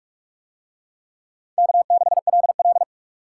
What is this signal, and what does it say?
Morse code, heard as a keyed tone, about 39 wpm, 700 Hz. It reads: K6LL